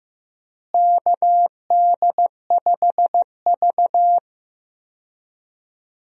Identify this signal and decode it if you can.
Morse code, heard as a keyed tone, about 15 wpm, 700 Hz. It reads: KD5V
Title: USA Call Signs - 4 Characters - Repeated 3x - 15wpm